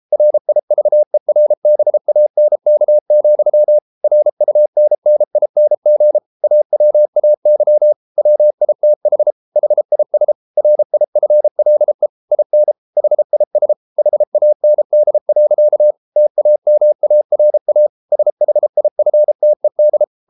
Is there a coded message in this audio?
Yes: Morse code from 0.1 to 20.0 s, heard as a keyed tone, about 33 wpm, 600 Hz. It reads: RIVERBANK, RUNNING AWAY WITH HIS RIFLE IN HIS HAND. TAMARA SHIFTED